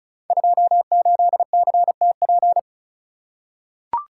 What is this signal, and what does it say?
Morse code, heard as a keyed tone, about 35 wpm, 700 Hz. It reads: 28CTP
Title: Code Groups - 5 Letter:Numbers - 35wpm